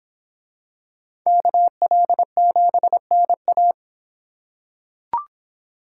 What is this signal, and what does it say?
Morse code, heard as a keyed tone, about 26 wpm, 700 Hz. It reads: KL7NA